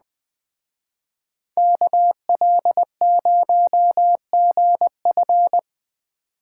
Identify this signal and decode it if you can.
Morse code, heard as a keyed tone, about 21 wpm, 700 Hz. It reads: KL0GF
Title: Speed Racing - USA 2x2 Call Signs 17wpm